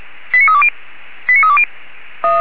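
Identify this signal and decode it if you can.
Five-tone selective call (ZVEI), heard as quick runs of five tones, tones of about 70 ms. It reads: 89320 89320
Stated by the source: Received on an SDR; ZVEI five-tone code 89315.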